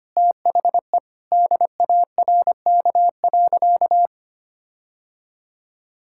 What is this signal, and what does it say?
Morse code, heard as a keyed tone, about 25 wpm, 700 Hz. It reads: THE DARK.
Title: Sentences from Top 1600 Words 25wpm